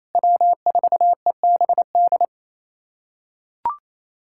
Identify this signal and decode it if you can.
Morse code, heard as a keyed tone, about 28 wpm, 700 Hz. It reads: W4EBD
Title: USA Callsign DE Callsign 28wpm